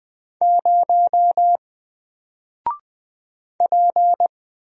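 Morse code, heard as a keyed tone, about 20 wpm, 700 Hz. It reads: 0 P
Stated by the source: Single letter and number Morse drill at 20 wpm, from N4LCD.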